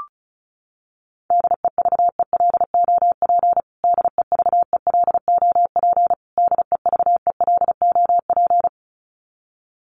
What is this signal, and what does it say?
Morse code, heard as a keyed tone, about 34 wpm, 700 Hz. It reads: DEVELOP DEVELOP DEVELOP
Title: Top 500 Words Repeated 35wpm